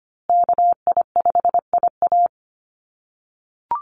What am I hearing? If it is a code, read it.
Morse code, heard as a keyed tone, about 25 wpm, 700 Hz. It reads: KI5IA
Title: USA Call Signs Repeated 3x 25wpm